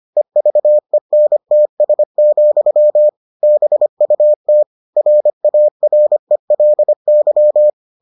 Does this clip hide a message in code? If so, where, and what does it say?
Yes: Morse code from 0.2 to 7.7 s, heard as a keyed tone, about 25 wpm, 600 Hz. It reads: EVENTS, BUT RARELY